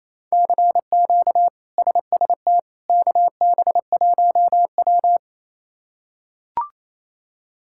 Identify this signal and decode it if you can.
Morse code, heard as a keyed tone, about 27 wpm, 700 Hz. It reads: CQ SST KB1W
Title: K1USN SST QSO Practice - 28wpm